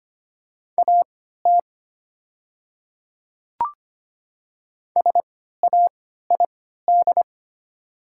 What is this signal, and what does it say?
Morse code, heard as a keyed tone, about 25 wpm, spaced as for 14 wpm, 700 Hz. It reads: AT SAID
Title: 3x Character Spacing - Top 300 Words - 25wpm